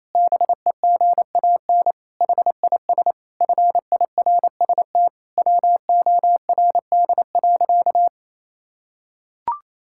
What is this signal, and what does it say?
Morse code, heard as a keyed tone, about 27 wpm, 700 Hz. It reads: BEGAN HIS FIRST WORD.
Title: Pangram Challenge v2 - 28wpm